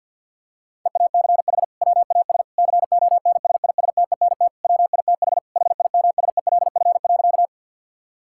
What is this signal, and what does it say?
Morse code, heard as a keyed tone, about 51 wpm, 700 Hz. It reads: EACH WAS CONSISTENT WITH HIMSELF.
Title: Sentences from Top 1600 Words 50wpm